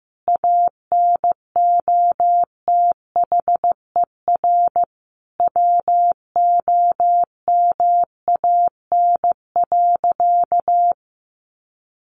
Morse code, heard as a keyed tone, about 15 wpm, 700 Hz. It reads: ANOTHER WOMAN.
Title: Sentences from Top 1400 Words 15wpm